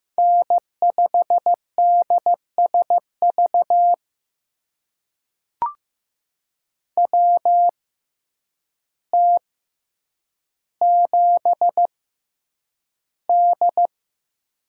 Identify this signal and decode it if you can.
Morse code, heard as a keyed tone, about 15 wpm, spaced as for 8 wpm, 700 Hz. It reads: N5DSV WT7D
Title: USA Call Signs - 6x Character Spacing - 15wpm